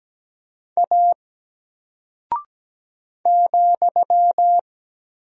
Morse code, heard as a keyed tone, about 17 wpm, 700 Hz. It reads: A ,
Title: Single Character (Letter:Number:Punctuation:Prosign) Alt 17wpm